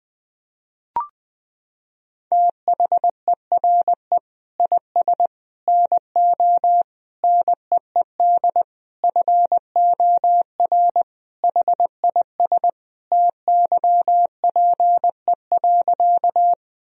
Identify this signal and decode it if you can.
Morse code, heard as a keyed tone, about 20 wpm, 700 Hz. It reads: THERE IS NO NEED FOR HIS TYPE.
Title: Sentences from Top 1000 Words - Review Part 3 - 20wpm